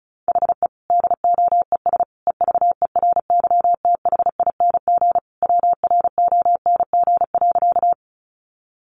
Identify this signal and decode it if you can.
Morse code, heard as a keyed tone, about 34 wpm, 700 Hz. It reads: HE DOES EVERYTHING WRONG.
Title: Sentences from Top 900 Words 35wpm